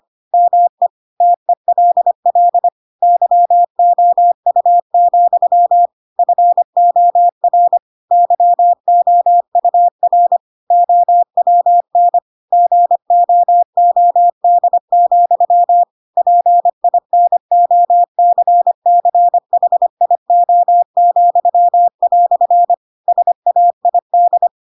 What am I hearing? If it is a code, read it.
Morse code, heard as a keyed tone, about 25 wpm, 700 Hz. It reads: ME TELL YOU, FOR YOUR OWN GOOD, PINOCCHIO," SAID
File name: Pinocchio / Pinocchio ch4 25 25 ik1hge